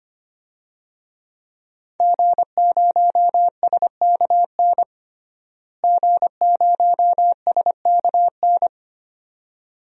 Morse code, heard as a keyed tone, about 25 wpm, 700 Hz. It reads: G0SKN G0SKN
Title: DXCC-100 Call Signs - 5 Characters - Repeated 2x - 25wpm